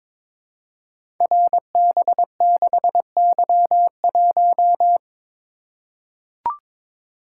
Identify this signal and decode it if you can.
Morse code, heard as a keyed tone, about 22 wpm, 700 Hz. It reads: RB6Y1
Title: Code Groups - 5 Letter:Numbers - 22wpm